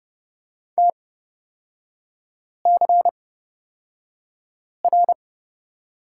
Morse code, heard as a keyed tone, about 30 wpm, spaced as for 5 wpm, 700 Hz. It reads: TCR
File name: Lesson 36e - BT - Three Characters 30wpm